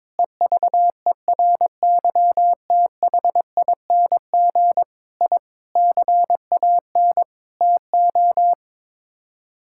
Morse code, heard as a keyed tone, about 22 wpm, 700 Hz. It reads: EVERYTHING I CAN TO
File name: practice / 22wpm / Sets of 8 Words - 22wpm